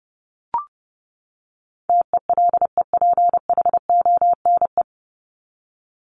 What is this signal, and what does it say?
Morse code, heard as a keyed tone, about 30 wpm, 700 Hz. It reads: TELEPHONE
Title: Top 1501-1750 Words 30wpm